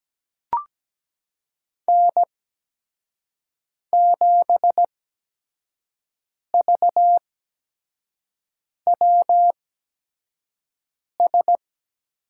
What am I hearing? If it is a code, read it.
Morse code, heard as a keyed tone, about 17 wpm, spaced as for 5 wpm, 700 Hz. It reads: N7VWS
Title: USA Call Signs - 5 Characters - 8x Character Spacing - 17wpm